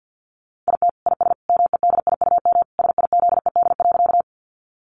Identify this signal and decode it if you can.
Morse code, heard as a keyed tone, about 50 wpm, 700 Hz. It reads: IT IS MEDIUM SIZED.
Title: Easy Phrases 50wpm